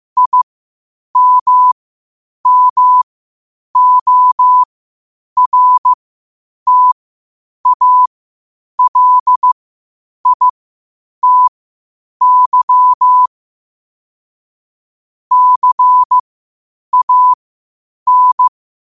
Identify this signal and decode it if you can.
Morse code, heard as a keyed tone, about 15 wpm, spaced as for 8 wpm, 1000 Hz. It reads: IMMORTALITY CAN
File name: qotdfiles / 8wpm